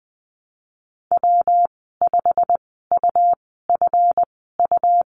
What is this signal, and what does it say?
Morse code, heard as a keyed tone, about 20 wpm, spaced as for 15 wpm, 700 Hz. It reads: W5UFU
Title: 193 Lesson 31g - Letter v - Call Signs 20wpm